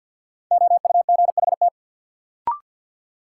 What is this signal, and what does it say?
Morse code, heard as a keyed tone, about 53 wpm, 700 Hz. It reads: OUGHT
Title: Sets of 5 Words Encore - New Word Drill - 50wpm